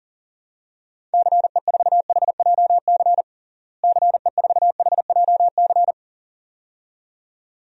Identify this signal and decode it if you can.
Morse code, heard as a keyed tone, about 40 wpm, 700 Hz. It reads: CE4HJC CE4HJC
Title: DXCC-100 Call Signs - 6 Characters - Repeated 2x - 40wpm